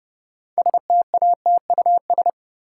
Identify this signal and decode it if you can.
Morse code, heard as a keyed tone, about 30 wpm, 700 Hz. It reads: STATUS
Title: Sentences from Top 1400 Words - New Word Drill Alt - 30wpm